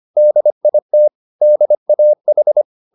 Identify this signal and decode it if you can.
Morse code, heard as a keyed tone, about 25 wpm, 600 Hz. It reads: DIT DAH